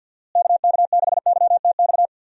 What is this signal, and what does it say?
Morse code, heard as a keyed tone, about 52 wpm, 700 Hz. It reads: KK6YTX
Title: USA Call Signs - 6 Characters - Repeated 2x - 50wpm